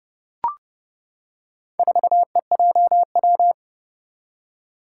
Morse code, heard as a keyed tone, about 28 wpm, 700 Hz. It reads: 4EJW
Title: Four Challenging Letter-Number Combinations 30wpm